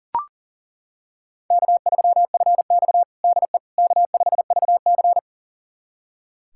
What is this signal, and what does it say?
Morse code, heard as a keyed tone, about 40 wpm, 700 Hz. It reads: K3FX DE K5VC
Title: USA Callsign DE Callsign - 4 Characters - 40wpm